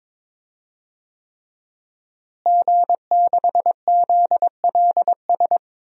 Morse code, heard as a keyed tone, about 22 wpm, 700 Hz. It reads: G6ZLS
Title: DXCC-100 Call Signs - 5 Characters - Repeated 3x - 22wpm